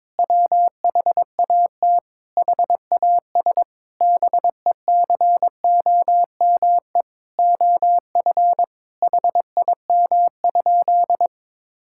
Morse code, heard as a keyed tone, about 22 wpm, 700 Hz. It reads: WHAT HAS BECOME OF HIM?
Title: Sentences from Top 1000 Words - Review Part 5 - 22wpm